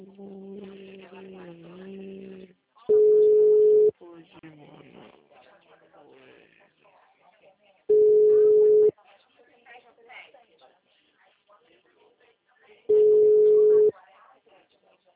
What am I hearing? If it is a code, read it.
Ringback tone (Europe, 425 Hz), with speech.